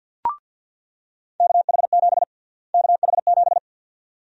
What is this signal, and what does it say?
Morse code, heard as a keyed tone, about 48 wpm, 700 Hz. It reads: KH7 KH7
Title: USA Call Sign Prefixes - Repeated 2x - 50wpm